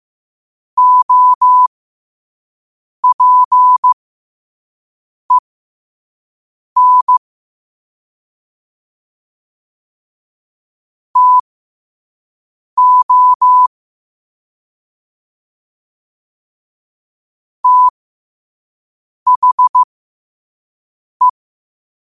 Morse code, heard as a keyed tone, about 15 wpm, spaced as for 5 wpm, 1000 Hz. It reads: OPEN TO THE
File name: qotdfiles / 5wpm